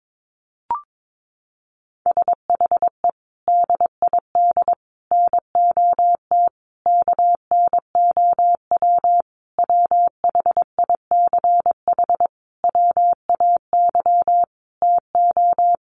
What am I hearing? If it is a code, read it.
Morse code, heard as a keyed tone, about 22 wpm, 700 Hz. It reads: SHE DID NOT KNOW WHICH WAY TO
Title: Sentences from Top 1100 Words 22wpm